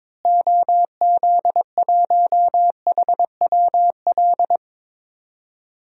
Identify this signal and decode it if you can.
Morse code, heard as a keyed tone, about 22 wpm, 700 Hz. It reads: OZ1HWL
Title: DXCC-100 Call Signs - 6 Characters - Repeated 3x - 22wpm